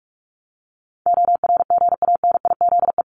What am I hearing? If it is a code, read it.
Morse code, heard as a keyed tone, about 44 wpm, 700 Hz. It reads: ORGANIZE
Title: Top 1751-2000 Words 45wpm